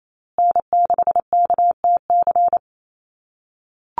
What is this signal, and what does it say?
Morse code, heard as a keyed tone, about 28 wpm, 700 Hz. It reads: N6KTC
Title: USA Call Signs Repeated 3x 28wpm